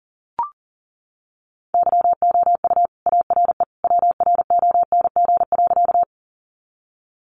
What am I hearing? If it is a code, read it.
Morse code, heard as a keyed tone, about 40 wpm, 700 Hz. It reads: YOU ARE WRONG.
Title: Sentences from Top 900 Words 40wpm